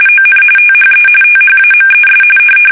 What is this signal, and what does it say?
RTTY, about 45.45 baud. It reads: ON THE THAMES
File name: fsk850 / usnavy3